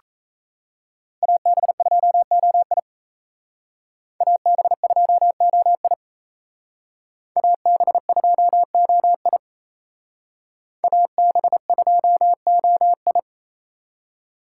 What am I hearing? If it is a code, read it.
Morse code, heard as a keyed tone, about 32 wpm, 700 Hz. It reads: AB2OI AB2OI AB2OI AB2OI
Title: Speed Racing - USA 2x2 Call Signs 28wpm